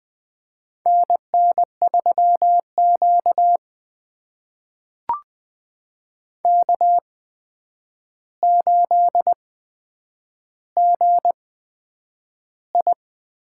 Morse code, heard as a keyed tone, about 20 wpm, spaced as for 9 wpm, 700 Hz. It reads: NN3Q K8GI
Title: USA Call Signs - 4 Characters - 8x Character Spacing - 20wpm